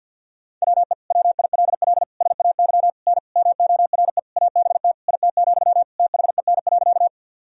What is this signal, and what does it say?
Morse code, heard as a keyed tone, about 51 wpm, 700 Hz. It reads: WE WILL SAY N MORE ABT IT, THEN.
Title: Sentences from Top 200 Words and Abbreviations 50wpm